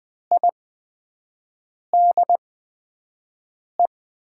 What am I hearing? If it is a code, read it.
Morse code, heard as a keyed tone, about 20 wpm, spaced as for 5 wpm, 700 Hz. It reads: IDE
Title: USA Call Signs - 5 Characters - 8x Character Spacing - 20wpm